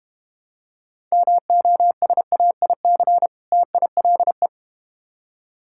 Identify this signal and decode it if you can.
Morse code, heard as a keyed tone, about 30 wpm, 700 Hz. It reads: MOSAIC TILE
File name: practice / 25wpm / Difficult Words - Speed-Racing with Hints - 25wpm